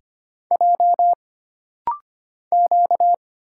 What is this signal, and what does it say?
Morse code, heard as a keyed tone, about 25 wpm, 700 Hz. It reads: J Q
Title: Single Letters - Rapid-Fire 25wpm